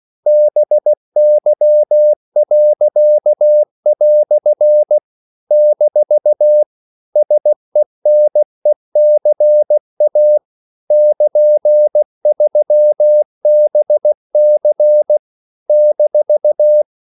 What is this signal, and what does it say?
Morse code, heard as a keyed tone, about 16 wpm, 600 Hz. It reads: BY." - SENECA <KN>3BC -
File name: CW / Practice / EnglishQuotations0216